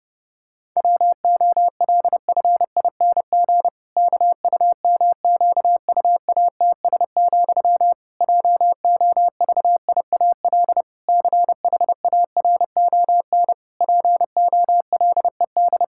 Morse code, heard as a keyed tone, about 32 wpm, 700 Hz. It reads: WOLFING KUMQUATS, JOVIAL CHARON POLED
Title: Pangram Challenge v2 - 30wpm